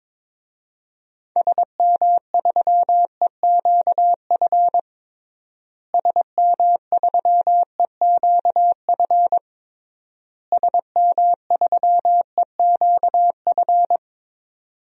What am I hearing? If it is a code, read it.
Morse code, heard as a keyed tone, about 22 wpm, 700 Hz. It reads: SM3EQF SM3EQF SM3EQF
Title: DXCC-100 Call Signs - 6 Characters - Repeated 3x - 22wpm